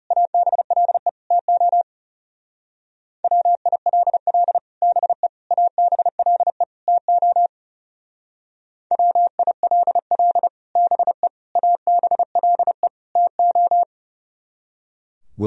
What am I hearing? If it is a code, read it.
Morse code, heard as a keyed tone, about 34 wpm, 700 Hz. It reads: ABLE TO WILL BE ABLE TO WILL BE ABLE TO
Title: Speed-Racing - Sets of 4 Words 30wpm